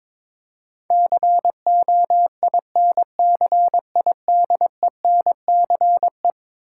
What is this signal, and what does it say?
Morse code, heard as a keyed tone, about 21 wpm, 700 Hz. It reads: COINCIDENCE
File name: practice / 22wpm / Eleven Letter Words - 22wpm